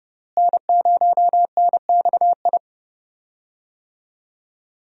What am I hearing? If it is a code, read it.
Morse code, heard as a keyed tone, about 30 wpm, 700 Hz. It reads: N0NXI
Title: USA Call Signs - 5 Characters - Repeated 3x - 30wpm